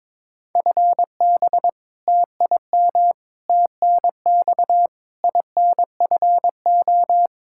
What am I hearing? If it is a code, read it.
Morse code, heard as a keyed tone, about 22 wpm, 700 Hz. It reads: FB TIM TNX INFO
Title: K1USN SST QSO Practice 22wpm